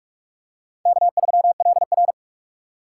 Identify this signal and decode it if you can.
Morse code, heard as a keyed tone, about 43 wpm, 700 Hz. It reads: K3LR
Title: K1USN SST QSO Practice - 45wpm